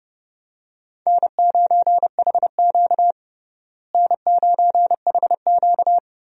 Morse code, heard as a keyed tone, about 30 wpm, 700 Hz. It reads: N9HQ N9HQ
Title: USA Call Signs - 4 Characters - Repeated 3x - 30wpm